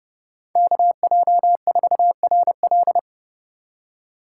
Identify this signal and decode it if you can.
Morse code, heard as a keyed tone, about 31 wpm, 700 Hz. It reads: KJ4RL